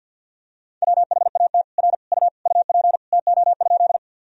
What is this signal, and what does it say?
Morse code, heard as a keyed tone, about 47 wpm, 700 Hz. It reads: WHAT R U UP TO?
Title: Sentences from Top 100 Words and Abbreviations 50wpm